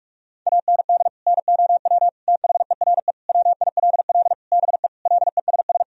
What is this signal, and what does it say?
Morse code, heard as a keyed tone, about 41 wpm, 700 Hz. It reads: AND NOW THERE WILL BE LESS
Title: Sentences from Top 1000 Words - Review Part 2 - 45wpm